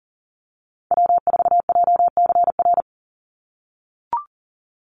Morse code, heard as a keyed tone, about 40 wpm, 690 Hz. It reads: W4JCR
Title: USA Call Signs 40wpm